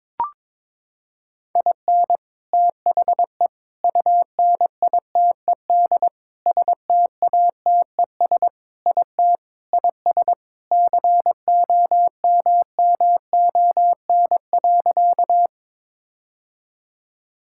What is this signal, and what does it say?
Morse code, heard as a keyed tone, about 22 wpm, 700 Hz. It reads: IN THE UNITED STATES IT IS COMMON.
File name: practice / 22wpm / Sentences with 4-Word Phrases - 22wpm